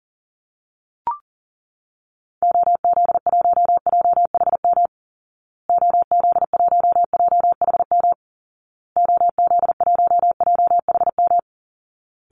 Morse code, heard as a keyed tone, about 40 wpm, 700 Hz. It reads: OZ1JHM OZ1JHM OZ1JHM
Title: Call Signs Repeated 2x - 40wpm